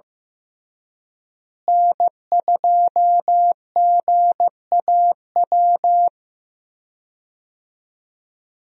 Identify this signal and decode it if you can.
Morse code, heard as a keyed tone, about 14 wpm, 700 Hz. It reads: N2GAW
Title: USA Call Signs - 5 Characters - Repeated 3x - 15wpm